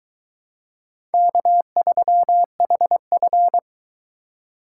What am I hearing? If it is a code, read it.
Morse code, heard as a keyed tone, about 22 wpm, 700 Hz. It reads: K3HF